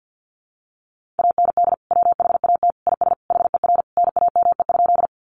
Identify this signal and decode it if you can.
Morse code, heard as a keyed tone, about 50 wpm, 700 Hz. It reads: AND WHAT IS HER NAME?
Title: Sentences from Top 200 Words 50wpm